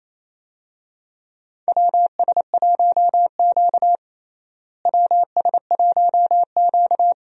Morse code, heard as a keyed tone, about 28 wpm, 700 Hz. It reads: WS1Q WS1Q
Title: USA Call Signs - 4 Characters - Repeated 3x - 28wpm